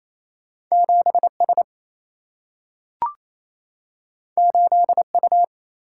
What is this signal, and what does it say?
Morse code, heard as a keyed tone, about 28 wpm, 700 Hz. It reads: 7S 8U
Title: Single-Letter Single-Number 28wpm